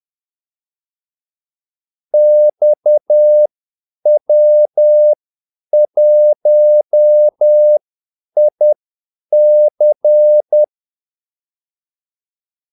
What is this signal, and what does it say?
Morse code, heard as a keyed tone, about 10 wpm, spaced as for 8 wpm, 600 Hz. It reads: XW1IC
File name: Callsigns-10-8wpm